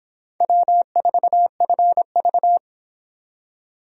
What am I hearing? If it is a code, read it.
Morse code, heard as a keyed tone, about 26 wpm, 700 Hz. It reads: W4FV